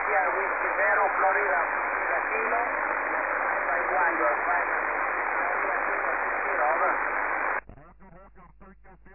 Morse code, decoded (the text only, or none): none